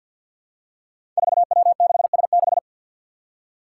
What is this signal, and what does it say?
Morse code, heard as a keyed tone, about 53 wpm, 700 Hz. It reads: 4W6S6
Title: Five Challenging Letter-Number Combinations 50wpm